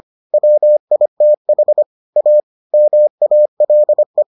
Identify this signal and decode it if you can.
Morse code, heard as a keyed tone, about 25 wpm, 600 Hz. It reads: WITH A MALE